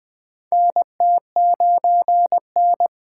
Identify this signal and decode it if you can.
Morse code, heard as a keyed tone, about 20 wpm, 700 Hz. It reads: NT9N
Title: USA Call Signs - 3x Character Spacing - 20wpm